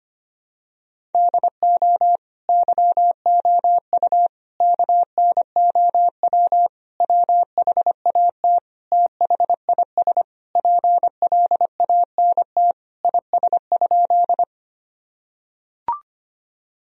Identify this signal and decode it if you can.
Morse code, heard as a keyed tone, about 25 wpm, 700 Hz. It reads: DO YOU KNOW WHAT THIS PLANT IS?
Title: Sentences from Top 1000 Words - Review Part 1 - 25wpm